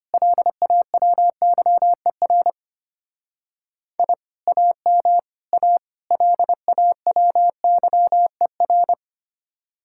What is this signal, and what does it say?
Morse code, heard as a keyed tone, about 33 wpm, 700 Hz. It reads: LAWYER I AM A LAWYER